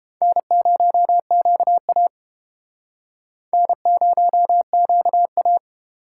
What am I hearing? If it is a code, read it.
Morse code, heard as a keyed tone, about 30 wpm, 700 Hz. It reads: N0QA N0QA